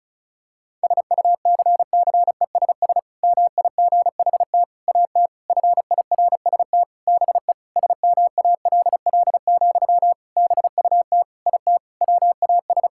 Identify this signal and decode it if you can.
Morse code, heard as a keyed tone, about 36 wpm, 700 Hz. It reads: SUCCESS MIGHT AT FIRST BE SMALL, BUT IT WAS
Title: Sentences from Top 1000 Words - Review Part 6 - 35wpm